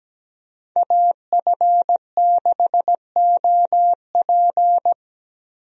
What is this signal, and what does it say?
Morse code, heard as a keyed tone, about 17 wpm, 700 Hz. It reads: AF6OP
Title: USA Call Signs - 5 Characters - Repeated 2x - 17wpm